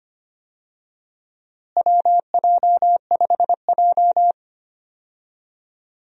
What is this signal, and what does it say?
Morse code, heard as a keyed tone, about 25 wpm, 700 Hz. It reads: WJ5J
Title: USA Call Signs - 4 Characters - Repeated 2x - 25wpm